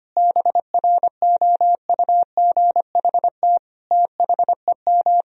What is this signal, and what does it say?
Morse code, heard as a keyed tone, about 25 wpm, 700 Hz. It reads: BROUGHT THEM
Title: Sentences from Top 1000 Words - Review Part 3 - 25wpm